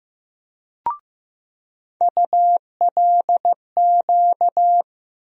Morse code, heard as a keyed tone, about 15 wpm, 700 Hz. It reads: ULQ
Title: USA Call Sign Postfixes - 3 Letters 15wpm